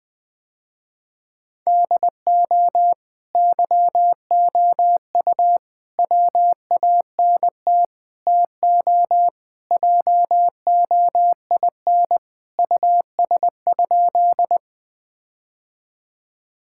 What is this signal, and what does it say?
Morse code, heard as a keyed tone, about 19 wpm, 700 Hz. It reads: DO YOU WANT TO JOIN US?
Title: Sentences with 4-Word Phrases - Spoken Beginning - 20wpm